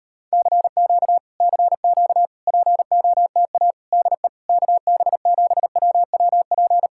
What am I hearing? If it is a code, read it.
Morse code, heard as a keyed tone, about 38 wpm, 700 Hz. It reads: CQ CQ POTA DE KB7WWP